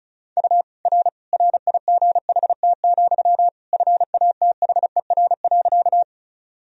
Morse code, heard as a keyed tone, about 34 wpm, 700 Hz. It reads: U R RIGHT, FATHER.
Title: Sentences from Top 300 Words and Abbreviations 35wpm